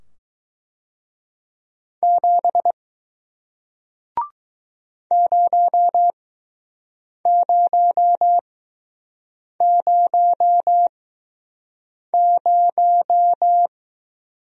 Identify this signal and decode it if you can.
Morse code, heard as a keyed tone, about 20 wpm, spaced as for 6 wpm, 700 Hz. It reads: 7 0000